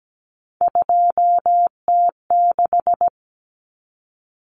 Morse code, heard as a keyed tone, about 17 wpm, 700 Hz. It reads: 2T6
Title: Common Cut Numbers - 3-Digit 17wpm